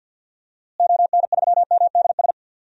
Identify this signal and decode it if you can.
Morse code, heard as a keyed tone, about 50 wpm, 700 Hz. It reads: ON3MDS
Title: DXCC-100 Call Signs - 6 Characters - Repeated 2x - 50wpm